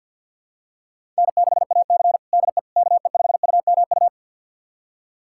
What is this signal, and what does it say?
Morse code, heard as a keyed tone, about 52 wpm, 700 Hz. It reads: N6AC DE KE5UMU
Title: USA Callsign DE Callsign 50wpm